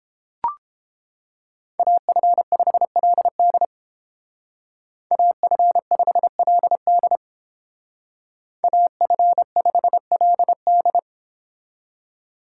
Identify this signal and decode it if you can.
Morse code, heard as a keyed tone, about 31 wpm, 700 Hz. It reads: AF5LD AF5LD AF5LD